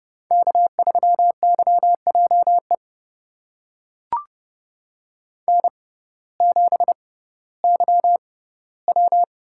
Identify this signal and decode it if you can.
Morse code, heard as a keyed tone, about 31 wpm, spaced as for 17 wpm, 700 Hz. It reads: K3YJE N7YW
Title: USA Call Signs - 5 Characters - 6x Character Spacing - 30wpm